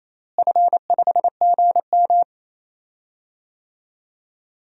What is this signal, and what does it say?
Morse code, heard as a keyed tone, about 28 wpm, 700 Hz. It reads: F5GM
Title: DXCC-100 Call Signs - 4 Characters - Repeated 2x - 28wpm